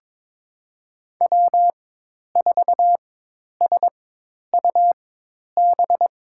Morse code, heard as a keyed tone, about 22 wpm, spaced as for 10 wpm, 700 Hz. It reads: W4SUB
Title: USA Call Signs - 4x Character Spacing - 22wpm